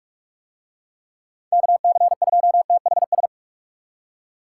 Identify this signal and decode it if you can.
Morse code, heard as a keyed tone, about 44 wpm, 700 Hz. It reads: KC2THS